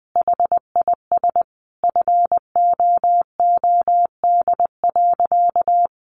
Morse code, heard as a keyed tone, about 20 wpm, 700 Hz. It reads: HIS FOOD.
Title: Sentences from Top 1100 Words 20wpm